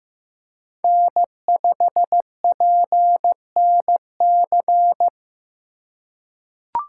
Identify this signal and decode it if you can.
Morse code, heard as a keyed tone, about 15 wpm, 700 Hz. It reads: N5PNC
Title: USA Call Signs - 4x Character Spacing - 15wpm